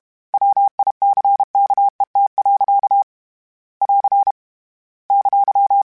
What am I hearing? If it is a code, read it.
Morse code, heard as a keyed tone, about 32 wpm, 800 Hz. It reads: WICKET. <AR> !
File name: Misc / Misc-mp3s / Word Pairs @30